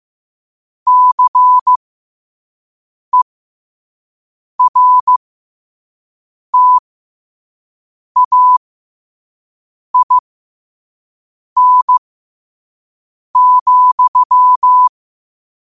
Morse code, heard as a keyed tone, about 15 wpm, spaced as for 5 wpm, 1000 Hz. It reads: CERTAIN,
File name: qotdfiles / 5wpm